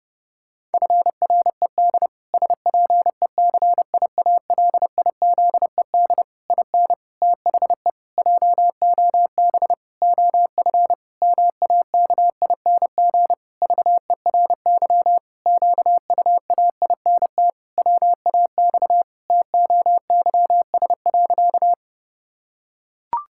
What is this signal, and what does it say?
Morse code, heard as a keyed tone, about 32 wpm, 700 Hz. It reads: FRED SPECIALIZED IN THE JOB OF MAKING VERY QUAINT WAX TOYS.
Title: Pangram Challenge v2 - 30wpm